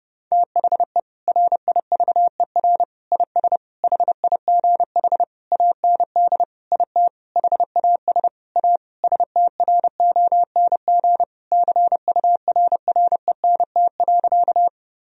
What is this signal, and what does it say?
Morse code, heard as a keyed tone, about 30 wpm, 700 Hz. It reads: THE RIVER IS HIGH AND IT HAS A STRONG CURRENT.
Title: Sentences from Top 1000 Words - Review Part 4 - 30wpm